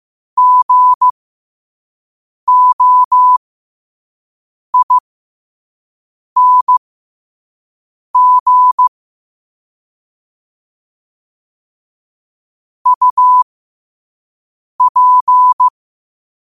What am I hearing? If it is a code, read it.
Morse code, heard as a keyed tone, about 15 wpm, spaced as for 5 wpm, 1000 Hz. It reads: GOING UP
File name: qotdfiles / 5wpm